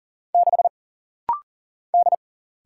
Morse code, heard as a keyed tone, about 43 wpm, 700 Hz. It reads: 6 D
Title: Single Letter-Number - Rapid-Fire 40wpm